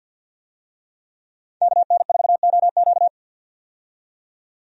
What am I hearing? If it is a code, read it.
Morse code, heard as a keyed tone, about 45 wpm, 700 Hz. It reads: KN4OQ
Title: USA Call Signs - 5 Characters - 50wpm